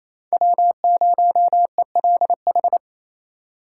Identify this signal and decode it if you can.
Morse code, heard as a keyed tone, about 27 wpm, 700 Hz. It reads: W0ELH
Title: USA Call Signs - 6x Character Spacing - 28wpm